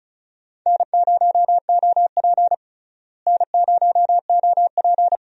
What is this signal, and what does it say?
Morse code, heard as a keyed tone, about 34 wpm, 700 Hz. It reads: N0OP N0OP
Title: USA Call Signs - 4 Characters - Repeated 2x - 35wpm